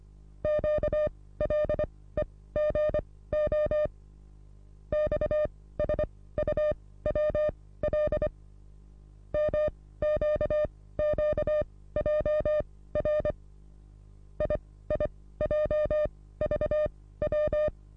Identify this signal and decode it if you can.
Morse code, heard as a keyed tone, about 24 wpm, spaced as for 16 wpm, 600 Hz. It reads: QLEGO XSUWL MQQJR IIJVW
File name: repos / Morsekurs / DJ4UF Morsen27